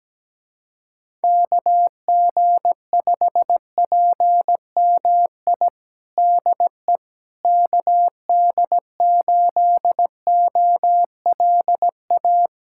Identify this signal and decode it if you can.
Morse code, heard as a keyed tone, about 17 wpm, 700 Hz. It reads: KG5PMI DE KD8OLA
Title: USA Callsign DE Callsign - 6 Characters - 17wpm